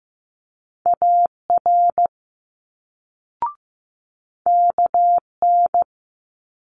Morse code, heard as a keyed tone, about 16 wpm, 700 Hz. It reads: AR KN
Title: Multi-letter Phonemes Quick Fire 15wpm